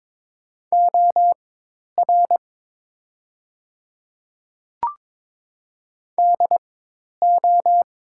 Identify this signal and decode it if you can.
Morse code, heard as a keyed tone, about 22 wpm, spaced as for 10 wpm, 700 Hz. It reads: OR DO
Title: 4x Character Spacing - Top 200 Words - 22wpm